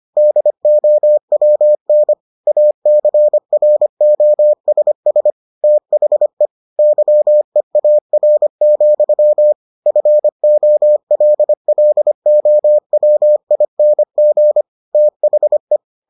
Morse code, heard as a keyed tone, about 25 wpm, 600 Hz. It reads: DOWN ACROSS THE YEAR, FOLLOWING THE